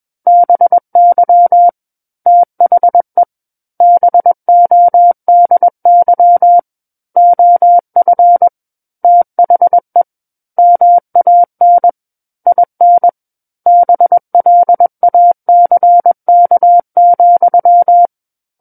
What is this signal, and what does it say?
Morse code, heard as a keyed tone, about 21 wpm, 700 Hz. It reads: BY THE BODY OF THE MAN IN BLACK,
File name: Book One / 12-Chapter 12 - 21 WPM